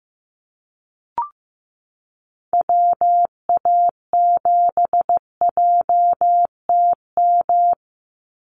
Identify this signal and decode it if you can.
Morse code, heard as a keyed tone, about 15 wpm, 700 Hz. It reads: WA7JTM
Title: NA SOTA Activator Callsigns 15wpm